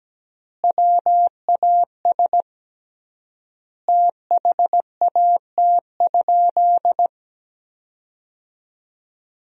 Morse code, heard as a keyed tone, about 17 wpm, 700 Hz. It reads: WAS THAT?
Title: 3x Word-Spacing - Sentences from Top 100 Words - 17wpm